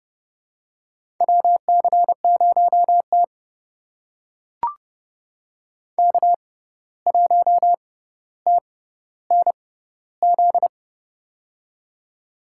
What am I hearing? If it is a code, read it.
Morse code, heard as a keyed tone, about 32 wpm, spaced as for 15 wpm, 700 Hz. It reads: WC0T K1TNZ
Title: USA Call Signs - 6x Character Spacing - 30wpm